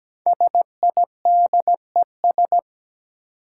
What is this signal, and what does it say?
Morse code, heard as a keyed tone, about 17 wpm, 700 Hz. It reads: SIDES
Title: Sets of 4 Words - New Word Drill - 17wpm